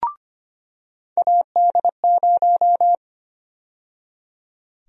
Morse code, heard as a keyed tone, about 25 wpm, 700 Hz. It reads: AD0
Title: USA Call Sign Prefixes - 25wpm